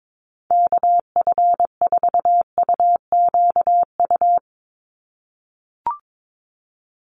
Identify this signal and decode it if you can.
Morse code, heard as a keyed tone, about 22 wpm, 700 Hz. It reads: KF4UQU
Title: USA Call Signs 22wpm